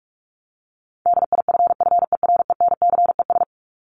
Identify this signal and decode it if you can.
Morse code, heard as a keyed tone, about 44 wpm, 700 Hz. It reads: DIFFERENCES
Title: ES:S Suffix - Top 100 Words with Base Words 45wpm